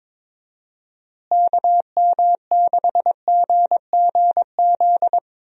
Morse code, heard as a keyed tone, about 22 wpm, 700 Hz. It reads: KM6GGZ
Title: USA Call Signs - 6 Characters - Repeated 2x - 22wpm